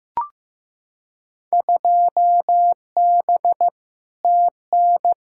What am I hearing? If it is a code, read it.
Morse code, heard as a keyed tone, about 15 wpm, 700 Hz. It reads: 2B TN
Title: Field Day Exchanges Encore - 15wpm